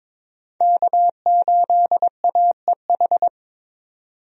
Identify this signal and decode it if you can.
Morse code, heard as a keyed tone, about 22 wpm, 700 Hz. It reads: K8AEH
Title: USA Call Signs - 5 Characters - 22wpm